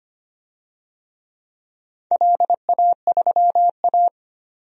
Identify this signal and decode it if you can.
Morse code, heard as a keyed tone, about 25 wpm, 700 Hz. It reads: LA3A
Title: DXCC-100 Call Signs - 4 Characters - 25wpm